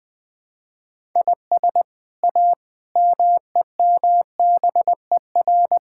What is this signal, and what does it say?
Morse code, heard as a keyed tone, about 20 wpm, 700 Hz. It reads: IS A MEMBER
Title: Build A Phrase - 5-Word Phrases - 20wpm